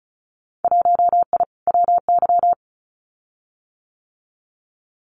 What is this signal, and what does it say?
Morse code, heard as a keyed tone, about 36 wpm, 700 Hz. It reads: 1I WY
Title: Winter Field Day Exchanges Encore 35wpm